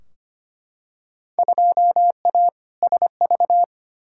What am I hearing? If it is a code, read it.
Morse code, heard as a keyed tone, about 25 wpm, 700 Hz. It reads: 2A SV
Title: Field Day Exchanges - 25wpm